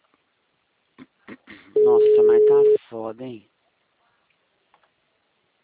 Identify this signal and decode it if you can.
Ringback tone (Europe, 425 Hz), with speech.